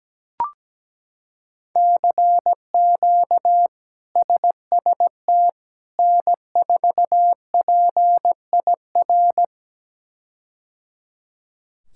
Morse code, heard as a keyed tone, about 17 wpm, 700 Hz. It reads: CQ SST N4PIR
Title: K1USN SST QSO Practice - 17wpm